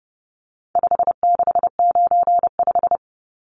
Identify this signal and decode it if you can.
Morse code, heard as a keyed tone, about 30 wpm, 700 Hz. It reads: 5695
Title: Numbers - 4-Digit 30wpm